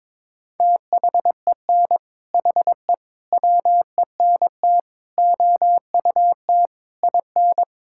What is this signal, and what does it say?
Morse code, heard as a keyed tone, about 22 wpm, 700 Hz. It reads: THEN HE WENT OUT IN
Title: Sentences from Top 1000 Words - Review Part 5 - 22wpm